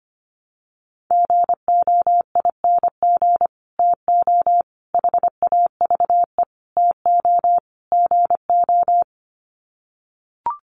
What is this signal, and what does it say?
Morse code, heard as a keyed tone, about 25 wpm, 690 Hz. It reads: GOING TO HAVE TO GO
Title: Sets of 5 Words Encore 25wpm